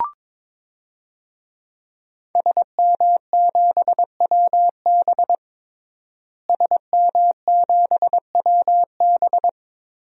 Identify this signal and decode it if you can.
Morse code, heard as a keyed tone, about 22 wpm, 700 Hz. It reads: SM7WB SM7WB
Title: DXCC-100 Call Signs - 5 Characters - Repeated 2x - 22wpm